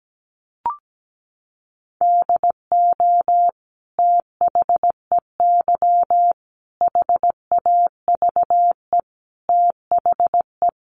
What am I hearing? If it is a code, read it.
Morse code, heard as a keyed tone, about 17 wpm, 700 Hz. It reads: DO THEY HAVE THE
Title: Sentences from Top 1300 Words 17wpm